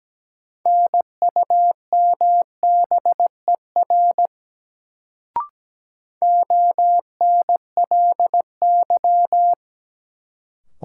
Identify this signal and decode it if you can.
Morse code, heard as a keyed tone, about 17 wpm, 700 Hz. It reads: NUMBER ONLY
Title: Sets of 2 Words - New Word Drill - 17wpm